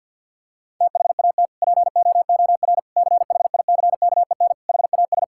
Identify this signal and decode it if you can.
Morse code, heard as a keyed tone, about 50 wpm, 700 Hz. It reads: THAT POOR CHICKEN HAS
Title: Sentences from Top 1000 Words - Review Part 5 - 50wpm